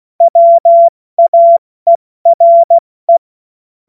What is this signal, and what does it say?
Morse code, heard as a keyed tone, about 16 wpm, 680 Hz. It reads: WAERE
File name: Morsekurs / Witze / Witz 18. Juli 2023 70 F80